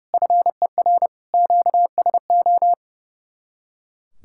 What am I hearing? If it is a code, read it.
Morse code, heard as a keyed tone, about 37 wpm, 700 Hz. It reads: FER QSO
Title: Speed-Racing Common QSO Phrases - 30wpm